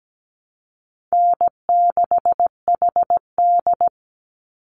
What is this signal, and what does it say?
Morse code, heard as a keyed tone, about 17 wpm, 700 Hz. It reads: N6HD
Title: Easy Call Signs Repeated 3x - 17wpm